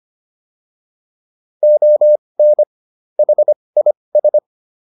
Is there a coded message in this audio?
Yes: Morse code from 1.6 to 4.4 s, heard as a keyed tone, about 25 wpm, spaced as for 20 wpm, 600 Hz. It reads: ON HIS